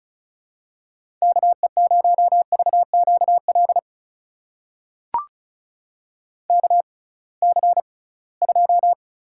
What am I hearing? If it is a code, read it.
Morse code, heard as a keyed tone, about 36 wpm, spaced as for 23 wpm, 690 Hz. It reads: KE0VQL KC2
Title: USA Call Signs - 6 Characters - 6x Character Spacing - 35wpm